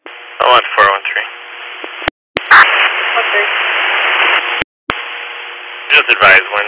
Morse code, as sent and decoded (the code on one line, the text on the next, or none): none